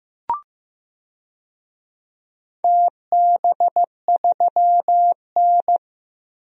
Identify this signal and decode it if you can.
Morse code, heard as a keyed tone, about 15 wpm, 700 Hz. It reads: TB3N
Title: DXCC-100 Call Signs - 4 Characters - Repeated 2x - 15wpm